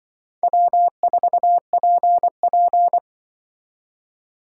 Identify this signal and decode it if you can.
Morse code, heard as a keyed tone, about 23 wpm, 700 Hz. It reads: W4PP
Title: Speed Racing - USA 1x2 Call Signs 20wpm